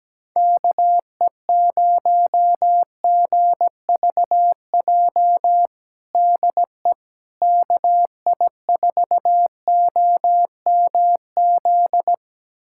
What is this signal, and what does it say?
Morse code, heard as a keyed tone, about 17 wpm, 700 Hz. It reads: KE0GVJ DE KI4OMZ
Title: USA Callsign DE Callsign 17wpm